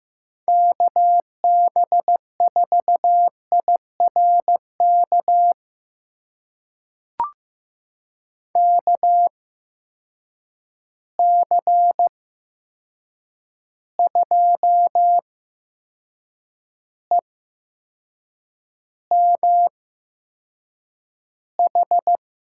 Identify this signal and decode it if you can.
Morse code, heard as a keyed tone, about 15 wpm, spaced as for 6 wpm, 700 Hz. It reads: KB4IRK KC2EMH
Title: USA Call Signs - 6 Characters - 8x Character Spacing - 15wpm